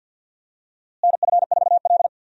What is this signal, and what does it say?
Morse code, heard as a keyed tone, about 50 wpm, 700 Hz. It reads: NF4L
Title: USA Call Signs - 4 Characters - Repeated 3x - 50wpm